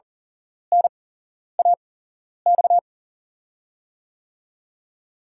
Morse code, heard as a keyed tone, about 38 wpm, spaced as for 11 wpm, 700 Hz. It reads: NAX
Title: USA Call Signs - 6 Characters - 8x Character Spacing - 40wpm